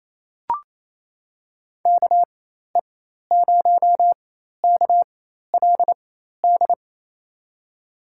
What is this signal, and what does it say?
Morse code, heard as a keyed tone, about 28 wpm, spaced as for 13 wpm, 700 Hz. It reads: KE0KLD
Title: USA Call Signs - 6 Characters - 4x Character Spacing - 28wpm